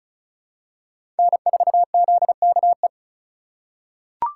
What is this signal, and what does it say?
Morse code, heard as a keyed tone, about 34 wpm, 700 Hz. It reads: N4ZKE
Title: USA Call Signs - 5 Characters - 3x Character Spacing - 35wpm